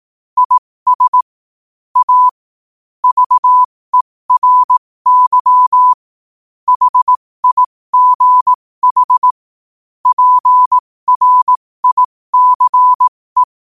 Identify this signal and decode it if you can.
Morse code, heard as a keyed tone, about 18 wpm, spaced as for 15 wpm, 1000 Hz. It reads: IS A VERY HIGH PRICE
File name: qotdfiles / 15wpm